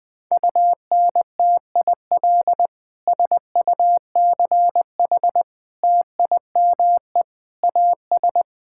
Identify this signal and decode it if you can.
Morse code, heard as a keyed tone, about 20 wpm, 700 Hz. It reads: UNTIL SUCH TIME AS
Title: Sets of 9 Words - 20wpm